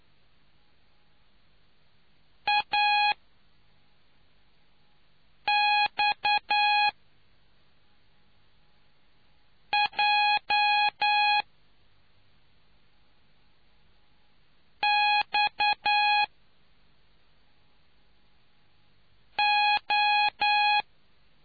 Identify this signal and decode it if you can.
Morse code, heard as a keyed tone, about 9 wpm, spaced as for 3 wpm, 830 Hz. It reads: AXJXO